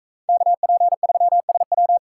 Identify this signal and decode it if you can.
Morse code, heard as a keyed tone, about 44 wpm, 700 Hz. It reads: KP3SW